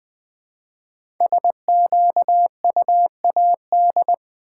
Morse code, heard as a keyed tone, about 20 wpm, 700 Hz. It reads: SQUAD
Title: 5 Letter Words - First 3 Letters Given - 20wpm